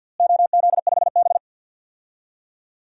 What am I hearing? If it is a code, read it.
Morse code, heard as a keyed tone, about 52 wpm, 700 Hz. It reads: OZ5B
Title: DXCC-100 Call Signs - 4 Characters - Repeated 3x - 50wpm